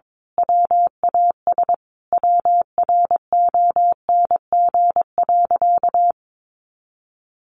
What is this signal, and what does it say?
Morse code, heard as a keyed tone, about 22 wpm, 700 Hz. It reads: WAS WRONG.
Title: Sentences from Top 1200 Words 22wpm